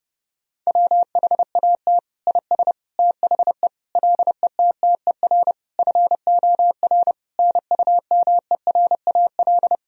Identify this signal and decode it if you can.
Morse code, heard as a keyed tone, about 32 wpm, 700 Hz. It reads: WHAT IS THE LETTER FOR NUMERAL
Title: Sentences from Top 1000 Words - Review Part 2 - 30wpm